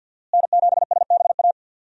Morse code, heard as a keyed tone, about 52 wpm, 700 Hz. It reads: N7SBA
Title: USA Call Signs - 5 Characters - Repeated 3x - 50wpm